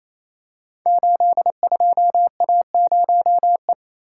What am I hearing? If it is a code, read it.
Morse code, heard as a keyed tone, about 28 wpm, 700 Hz. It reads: 82A0E
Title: Code Groups - 5 Letter:Numbers - 28wpm